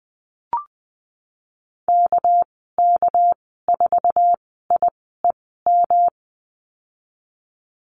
Morse code, heard as a keyed tone, about 20 wpm, spaced as for 15 wpm, 700 Hz. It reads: KK4IEM
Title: 219 Lesson 35g - Number 0 - Call Signs 20wpm